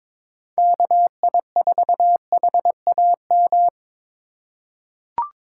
Morse code, heard as a keyed tone, about 22 wpm, 700 Hz. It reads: KI4HAM
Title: USA Call Signs - 6 Characters - 3x Character Spacing - 22wpm